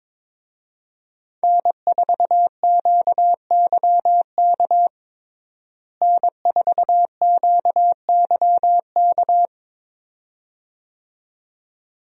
Morse code, heard as a keyed tone, about 21 wpm, 700 Hz. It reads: N4QYK N4QYK
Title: USA Call Signs - 5 Characters - Repeated 2x - 22wpm